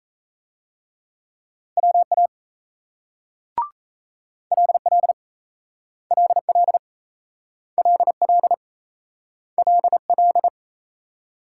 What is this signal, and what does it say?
Morse code, heard as a keyed tone, about 38 wpm, 700 Hz. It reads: WA LL LL LL LL